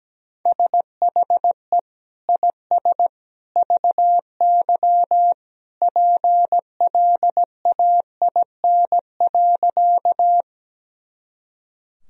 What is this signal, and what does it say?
Morse code, heard as a keyed tone, about 17 wpm, 700 Hz. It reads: SHE IS VY PLAIN.
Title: Sentences from Top 400 Words and Abbreviations 17wpm